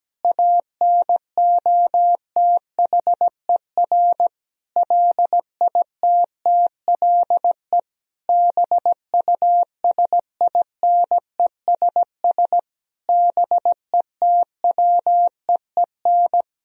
Morse code, heard as a keyed tone, about 17 wpm, 700 Hz. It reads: ANOTHER LITTLE BUSINESS BETWEEN
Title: Sentences from Top 1000 Words - Review Part 1 - 17wpm